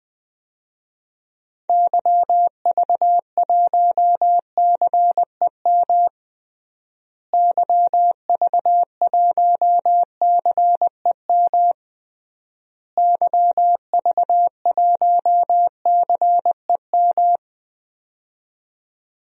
Morse code, heard as a keyed tone, about 20 wpm, 700 Hz. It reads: YV1CEM YV1CEM YV1CEM
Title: DXCC-100 Call Signs - 6 Characters - Repeated 3x - 20wpm